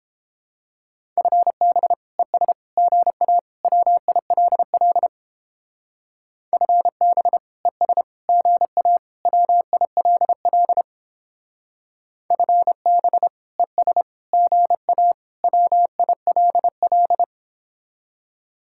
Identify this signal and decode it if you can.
Morse code, heard as a keyed tone, about 29 wpm, 700 Hz. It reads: FB ES GA WILL FB ES GA WILL FB ES GA WILL